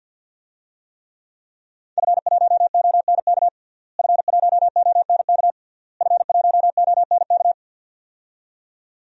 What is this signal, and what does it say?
Morse code, heard as a keyed tone, about 50 wpm, 700 Hz. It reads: F1ONK F1ONK F1ONK
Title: DXCC-100 Call Signs - 5 Characters - Repeated 3x - 50wpm